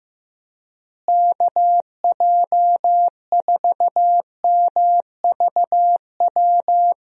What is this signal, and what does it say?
Morse code, heard as a keyed tone, about 15 wpm, 700 Hz. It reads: KJ4MVW
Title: USA Call Signs - 6 Characters - Repeated 3x - 15wpm